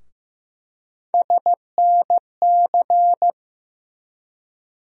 Morse code, heard as a keyed tone, about 15 wpm, 700 Hz. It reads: SNC
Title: USA Call Sign Postfixes - 3 Letters 15wpm